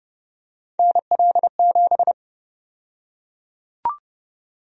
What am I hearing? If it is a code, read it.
Morse code, heard as a keyed tone, about 30 wpm, 700 Hz. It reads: NL7
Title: USA Call Sign Prefixes - 30wpm